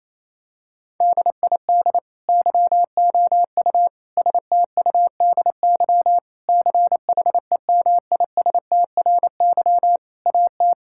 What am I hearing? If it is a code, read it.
Morse code, heard as a keyed tone, about 28 wpm, 700 Hz. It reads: DID YOU STUDY CHEMISTRY AT
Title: Sentences from Top 1000 Words - Review Part 6 - 28wpm